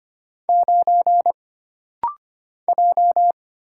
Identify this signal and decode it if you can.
Morse code, heard as a keyed tone, about 25 wpm, 700 Hz. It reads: 9 J
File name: Single Character (Letter:Number:Punctuation:Prosign) - Rapid-Fire - 25wpm